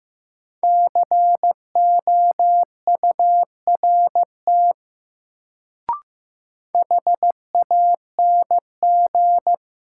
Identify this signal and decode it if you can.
Morse code, heard as a keyed tone, about 15 wpm, 700 Hz. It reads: COURT HANG
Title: Sentences from Top 1600 Words - New Word Drill - 15wpm